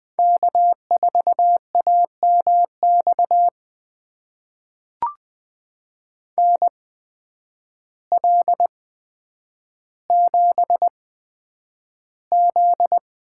Morse code, heard as a keyed tone, about 20 wpm, spaced as for 9 wpm, 700 Hz. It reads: K4AMX NL7Z
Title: USA Call Signs - 8x Character Spacing - 20wpm